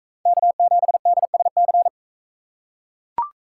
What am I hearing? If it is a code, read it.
Morse code, heard as a keyed tone, about 40 wpm, 700 Hz. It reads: K7DSC